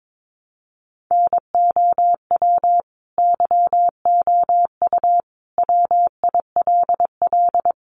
Morse code, heard as a keyed tone, about 22 wpm, 700 Hz. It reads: NOW YOU WILL
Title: Sentences from Top 1200 Words 22wpm